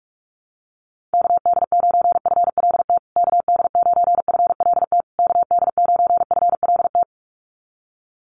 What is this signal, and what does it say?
Morse code, heard as a keyed tone, about 44 wpm, 700 Hz. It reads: KD9FLT KD9FLT KD9FLT
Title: USA Call Signs Repeated 3x 45wpm